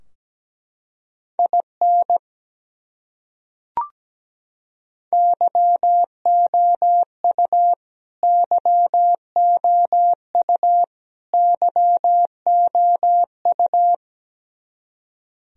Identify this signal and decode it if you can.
Morse code, heard as a keyed tone, about 17 wpm, 700 Hz. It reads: IN YOU YOU YOU
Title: Top 100 Words Repeated 17wpm